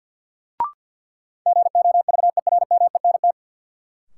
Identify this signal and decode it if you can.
Morse code, heard as a keyed tone, about 46 wpm, 700 Hz. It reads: GOVERMENT
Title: Sentences from Top 1300 Words - New Word Drill - 50wpm